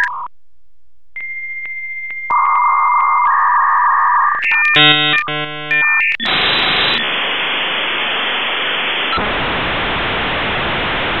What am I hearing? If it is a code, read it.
Dial-up modem handshake.